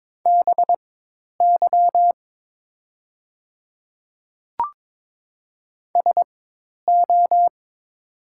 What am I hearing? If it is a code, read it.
Morse code, heard as a keyed tone, about 22 wpm, spaced as for 10 wpm, 700 Hz. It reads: BY SO
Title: 4x Character Spacing - Top 100 Words - 22wpm